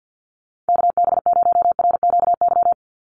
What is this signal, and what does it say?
Morse code, heard as a keyed tone, about 50 wpm, 700 Hz. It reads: KB0RQY
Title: USA Call Signs 50wpm